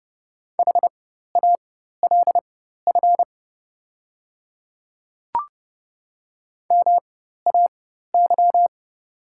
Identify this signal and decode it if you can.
Morse code, heard as a keyed tone, about 29 wpm, spaced as for 14 wpm, 700 Hz. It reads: HALF MAY
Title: 4x Character Spacing - Top 500 Words - 30wpm